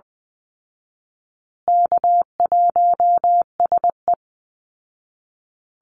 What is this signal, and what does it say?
Morse code, heard as a keyed tone, about 20 wpm, 700 Hz. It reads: K1SE
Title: Easy Call Signs Repeated 3x - 20wpm